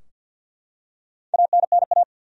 Morse code, heard as a keyed tone, about 52 wpm, 700 Hz. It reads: ANNA
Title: USA Common Names - 50wpm